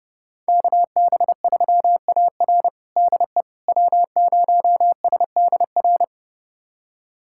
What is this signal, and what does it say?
Morse code, heard as a keyed tone, about 30 wpm, 700 Hz. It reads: KB3AR DE W0SDR
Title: USA Callsign DE Callsign - 5 Characters - 30wpm